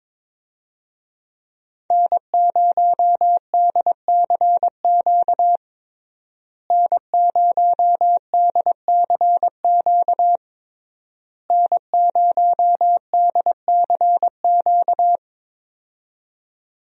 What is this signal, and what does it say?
Morse code, heard as a keyed tone, about 22 wpm, 700 Hz. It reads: N0DCQ N0DCQ N0DCQ
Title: USA Call Signs - 5 Characters - Repeated 3x - 22wpm